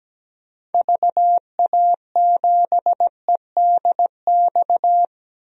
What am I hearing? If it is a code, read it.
Morse code, heard as a keyed tone, about 17 wpm, 700 Hz. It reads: VA7EDX
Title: DXCC-100 Call Signs - 6 Characters - 17wpm